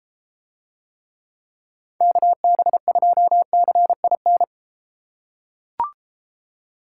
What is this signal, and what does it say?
Morse code, heard as a keyed tone, about 32 wpm, 700 Hz. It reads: KB2CIN